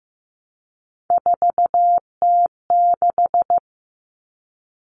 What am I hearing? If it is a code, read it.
Morse code, heard as a keyed tone, about 15 wpm, 700 Hz. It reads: 4T6
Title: Common Cut Numbers - 3-Digit 15wpm